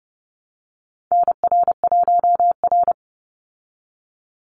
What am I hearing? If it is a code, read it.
Morse code, heard as a keyed tone, about 30 wpm, 700 Hz. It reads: NR1R
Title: Call Signs Repeated 2x - 30wpm